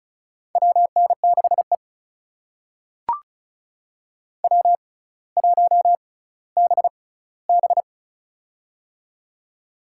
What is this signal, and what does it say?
Morse code, heard as a keyed tone, about 36 wpm, spaced as for 18 wpm, 700 Hz. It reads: WN6E W1BB
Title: USA Call Signs - 4 Characters - 6x Character Spacing - 35wpm